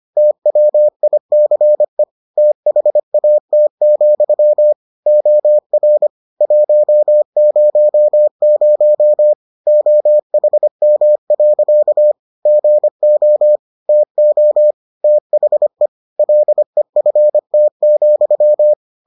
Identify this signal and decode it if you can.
Morse code, heard as a keyed tone, about 25 wpm, 600 Hz. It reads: TWICE THAT, OR 100 OHM. GO TO THE LEFT,